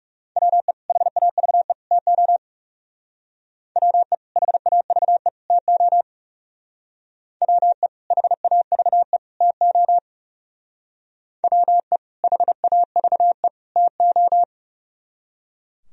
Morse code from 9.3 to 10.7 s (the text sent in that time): TO